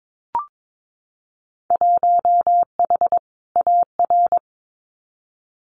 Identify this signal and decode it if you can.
Morse code, heard as a keyed tone, about 22 wpm, 700 Hz. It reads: 1H AR
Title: Winter Field Day Exchanges 22wpm